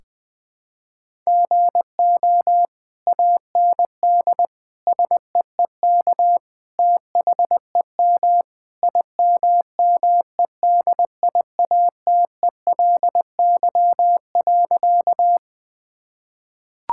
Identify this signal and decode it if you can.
Morse code, heard as a keyed tone, about 20 wpm, 700 Hz. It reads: GO AND SEEK THEM IMMEDIATELY.
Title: Sentences from Top 1600 Words 20wpm